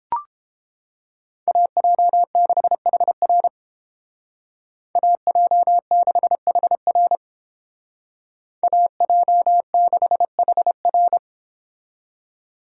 Morse code, heard as a keyed tone, about 29 wpm, 700 Hz. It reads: AJ6HR AJ6HR AJ6HR